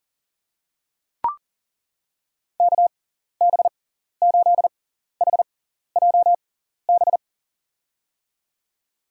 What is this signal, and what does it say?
Morse code, heard as a keyed tone, about 40 wpm, spaced as for 14 wpm, 700 Hz. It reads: KB8HJB
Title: USA Call Signs - 6 Characters - 6x Character Spacing - 40wpm